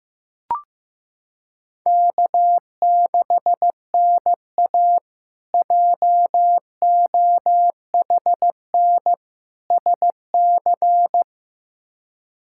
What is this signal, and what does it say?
Morse code, heard as a keyed tone, about 16 wpm, 700 Hz. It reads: K6NA JOHN SC
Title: K1USN SST QSO Practice - 15wpm